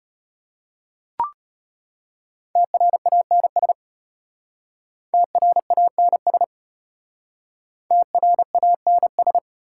Morse code, heard as a keyed tone, about 32 wpm, 700 Hz. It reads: TRANS TRANS TRANS